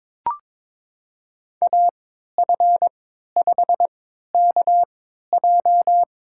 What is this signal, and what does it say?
Morse code, heard as a keyed tone, about 21 wpm, spaced as for 12 wpm, 700 Hz. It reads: AF5KJ
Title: USA Call Signs - 3x Character Spacing - 22wpm